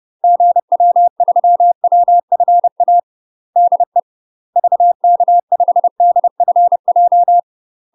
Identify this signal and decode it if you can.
Morse code, heard as a keyed tone, about 30 wpm, 700 Hz. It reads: GW3WFA DE VK5DFJ
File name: UK MASTER QSOs TRAINING / 30WPM / UK MASTER QSOs 0018